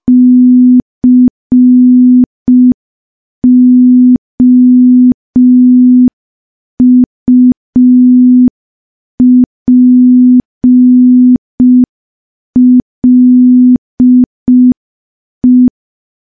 Morse code, heard as a keyed tone, about 5 wpm, 250 Hz. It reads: COUPLE
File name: morse / morse-training-05WPM